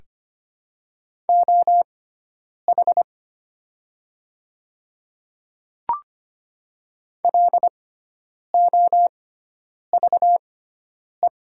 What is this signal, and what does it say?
Morse code, heard as a keyed tone, about 25 wpm, spaced as for 9 wpm, 700 Hz. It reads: OH LOVE
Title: 6x Character Spacing - Top 500 Words - 25wpm